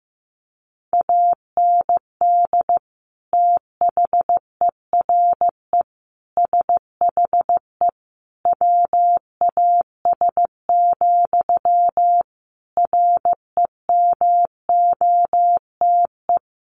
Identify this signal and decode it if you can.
Morse code, heard as a keyed tone, about 14 wpm, 700 Hz. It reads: AND THERE SHE WAS, REMOTE
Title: Sentences from Top 1300 Words 15wpm